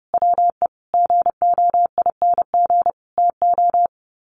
Morse code, heard as a keyed tone, about 30 wpm, 700 Hz. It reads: WE GOING TO
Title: Sets of 5 Words 30wpm